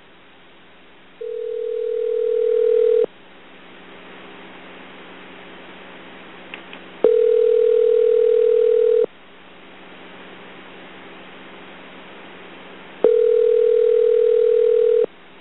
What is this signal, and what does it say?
Ringback tone (North America), 440 + 480 Hz.
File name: Medical / Recordings / McNoun c42-Varnau-ETA